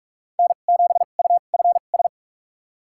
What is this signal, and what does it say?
Morse code, heard as a keyed tone, about 46 wpm, spaced as for 32 wpm, 700 Hz. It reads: N7UFS
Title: USA Call Signs - 5 Characters - 2x Character Spacing - 45wpm